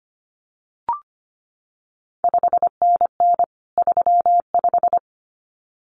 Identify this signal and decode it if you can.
Morse code, heard as a keyed tone, about 25 wpm, 700 Hz. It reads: 5NN 35
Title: CQ WW DX Contest Exchange Encore - 25wpm